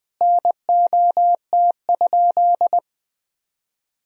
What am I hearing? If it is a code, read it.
Morse code, heard as a keyed tone, about 21 wpm, 700 Hz. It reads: NOT?
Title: Sentences from Top 1000 Words - Review Part 1 - 20wpm